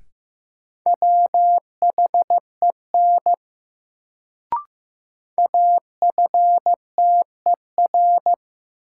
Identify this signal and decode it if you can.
Morse code, heard as a keyed tone, about 15 wpm, 700 Hz. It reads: WHEN AFTER
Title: Sets of 2 Words - New Word Drill - 15wpm